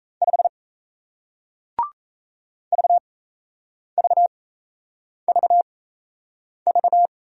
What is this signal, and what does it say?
Morse code, heard as a keyed tone, about 38 wpm, spaced as for 8 wpm, 700 Hz. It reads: 5 VVVV